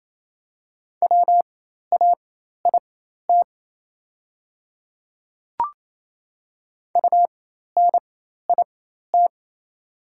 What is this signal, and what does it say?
Morse code, heard as a keyed tone, about 28 wpm, spaced as for 13 wpm, 700 Hz. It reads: WAIT UNIT
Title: 4x Character Spacing - Top 500 Words - 28wpm